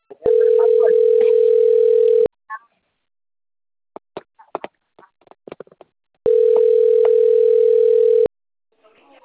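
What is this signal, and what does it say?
Ringback tone (North America), 440 + 480 Hz.
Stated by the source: Outbound telemarketing call recorded on a system labelled TLMK - Rio de Janeiro.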